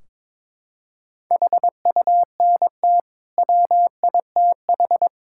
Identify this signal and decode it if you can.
Morse code, heard as a keyed tone, about 22 wpm, 700 Hz. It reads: HUNT WITH
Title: Sentences from Top 1600 Words 22wpm